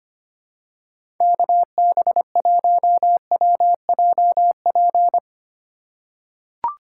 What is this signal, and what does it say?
Morse code, heard as a keyed tone, about 25 wpm, 700 Hz. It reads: KB1WJP
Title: USA Call Signs - 6 Characters - 3x Character Spacing - 25wpm